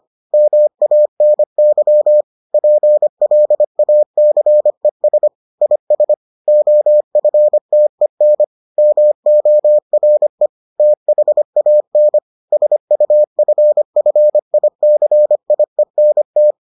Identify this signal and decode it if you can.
Morse code, heard as a keyed tone, about 25 wpm, 600 Hz. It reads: MANY PLACES IS OFTEN MORE THAN SUFFICIENT